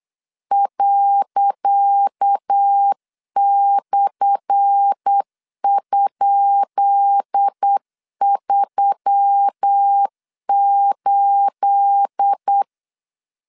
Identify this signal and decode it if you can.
Morse code, heard as a keyed tone, about 8 wpm, 800 Hz. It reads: ./?38
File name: LEZIONI / 10 SIGNS / 10C DOT 8wpm 3649